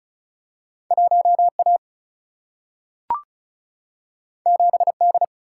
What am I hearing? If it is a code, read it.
Morse code, heard as a keyed tone, about 36 wpm, 700 Hz. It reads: 1 A 7 D
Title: Single-Letter Single-Number 35wpm